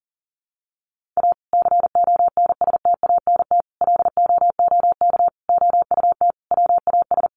Morse code, heard as a keyed tone, about 37 wpm, 700 Hz. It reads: A CONSTANT LOOK OUT WAS
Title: Sentences from Top 1500 Words 40wpm